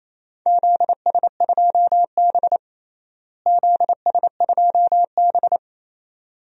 Morse code, heard as a keyed tone, about 28 wpm, 700 Hz. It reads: ZS2B ZS2B
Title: DXCC-100 Call Signs - 4 Characters - Repeated 2x - 28wpm